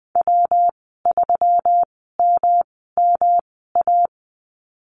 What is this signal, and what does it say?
Morse code, heard as a keyed tone, about 20 wpm, spaced as for 15 wpm, 700 Hz. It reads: W3MMA
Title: 154 Lesson 25g - Letter p - Call Signs 20wpm